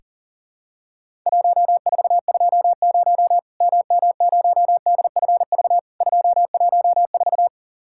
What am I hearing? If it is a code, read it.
Morse code, heard as a keyed tone, about 40 wpm, 700 Hz. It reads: 1420 MM0DFV 214
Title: Worked All Europe DX Contest - QTC Exchange - 40wpm